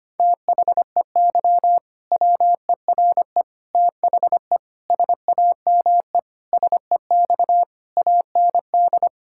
Morse code, heard as a keyed tone, about 25 wpm, 700 Hz. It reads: THEY WERE THE SAME SEX AND
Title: Sentences from Top 1600 Words 25wpm